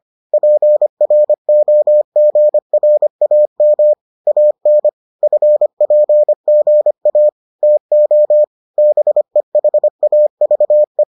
Morse code, heard as a keyed tone, about 25 wpm, 600 Hz. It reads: PROGRAM AN FPGA TO BEHAVE